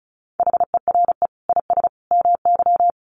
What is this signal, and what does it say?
Morse code, heard as a keyed tone, about 34 wpm, 700 Hz. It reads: HERE IS MY
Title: Sentences from Top 1400 Words 35wpm